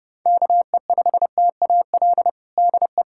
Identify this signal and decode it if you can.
Morse code, heard as a keyed tone, about 30 wpm, 690 Hz. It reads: KE5TAL DE
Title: USA Callsign DE Callsign - 6 Characters - 30wpm